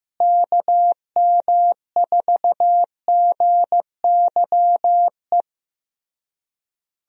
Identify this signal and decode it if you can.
Morse code, heard as a keyed tone, about 15 wpm, 700 Hz. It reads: KM4GYE
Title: USA Call Signs - 6 Characters - Repeated 3x - 15wpm